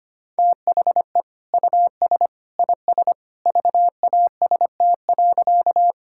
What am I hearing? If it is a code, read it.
Morse code, heard as a keyed tone, about 25 wpm, 700 Hz. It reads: THE US IS VAST.
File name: practice / 25wpm / Sentences with 2-Word Phrases - 25wpm